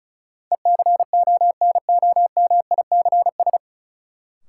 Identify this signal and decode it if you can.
Morse code, heard as a keyed tone, about 36 wpm, 700 Hz. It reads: ECONOMICS
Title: Sentences from Top 1000 Words - Review Part 3 - New Word Drill - 35wpm